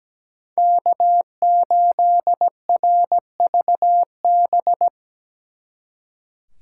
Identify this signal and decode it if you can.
Morse code, heard as a keyed tone, about 20 wpm, 700 Hz. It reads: K8RVB
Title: Speed Racing - USA 1x3 Call Signs 17wpm